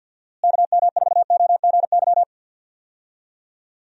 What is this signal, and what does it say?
Morse code, heard as a keyed tone, about 50 wpm, 700 Hz. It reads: KM4OGY
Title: USA Call Signs - 6 Characters - Repeated 3x - 50wpm